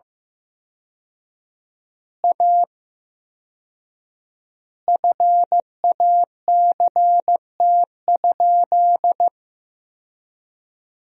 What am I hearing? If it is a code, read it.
Morse code, heard as a keyed tone, about 16 wpm, 700 Hz. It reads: A FACT?
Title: 4x Word-Spacing - Sentences from Top 500 Words - 15wpm